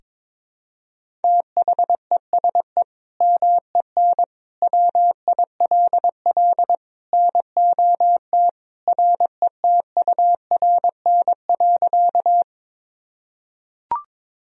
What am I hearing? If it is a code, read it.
Morse code, heard as a keyed tone, about 21 wpm, 700 Hz. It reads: THESE MEN WILL NOT RETURN.
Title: Sentences from Top 1000 Words - Review Part 1 - 22wpm